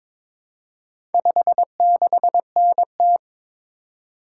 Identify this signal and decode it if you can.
Morse code, heard as a keyed tone, about 22 wpm, 700 Hz. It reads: 56NT
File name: practice / 22wpm / Four Letter-Number Combinations - 22wpm